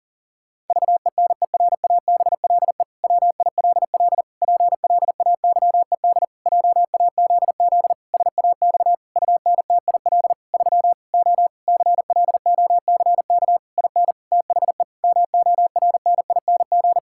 Morse code, heard as a keyed tone, about 37 wpm, 700 Hz. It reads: VENERABLE WILL PLAYED JAZZ SAX UNTIL 3 O CLOCK IN THE MORNING
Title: Pangram Challenge v2 - 40wpm